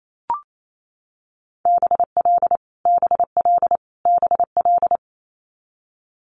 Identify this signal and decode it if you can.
Morse code, heard as a keyed tone, about 28 wpm, 700 Hz. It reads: BL BL BL
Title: Multi-letter Phonemes Repeated 3x 28wpm